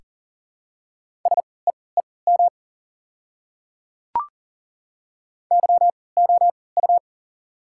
Morse code, heard as a keyed tone, about 39 wpm, spaced as for 22 wpm, 690 Hz. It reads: SEEM YOU
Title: 3x Character Spacing - Top 300 Words - 40wpm